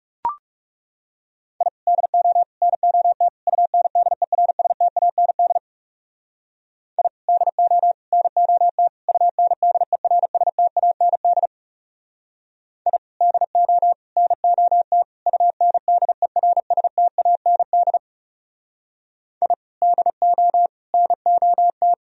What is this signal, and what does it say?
Morse code, heard as a keyed tone, about 36 wpm, 700 Hz. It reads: I DO NOT UNDERSTAND I DO NOT UNDERSTAND I DO NOT UNDERSTAND I DO NOT
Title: Speed-Racing - Sets of 4 Words 30wpm